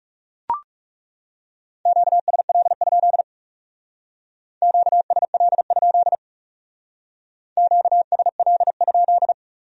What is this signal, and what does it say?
Morse code, heard as a keyed tone, about 37 wpm, 700 Hz. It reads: QSL? QSL? QSL?
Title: Speed-Racing Q-Codes - 30wpm